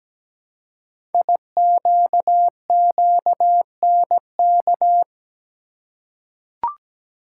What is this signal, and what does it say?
Morse code, heard as a keyed tone, about 17 wpm, 700 Hz. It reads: IQQNK
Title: Code Groups - 5 Letters - 17wpm